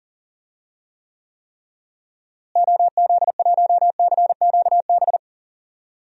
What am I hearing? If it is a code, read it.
Morse code, heard as a keyed tone, about 39 wpm, 700 Hz. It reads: OZ1CQB
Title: DXCC-100 Call Signs - 6 Characters - Repeated 3x - 40wpm